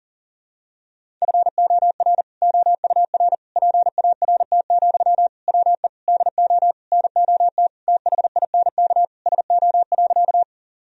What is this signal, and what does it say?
Morse code, heard as a keyed tone, about 40 wpm, 700 Hz. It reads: FOR OUR PART, WE DO NOT THINK SO.
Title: Sentences from Top 1000 Words - Review Part 1 - 40wpm